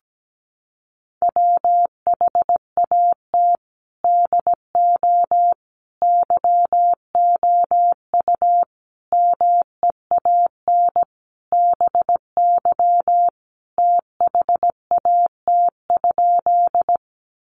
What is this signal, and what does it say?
Morse code, heard as a keyed tone, about 17 wpm, 700 Hz. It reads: WHAT DO YOU MEAN BY THAT?
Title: Sentences from Top 200 Words 17wpm